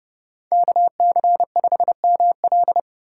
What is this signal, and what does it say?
Morse code, heard as a keyed tone, about 30 wpm, 690 Hz. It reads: KC5ML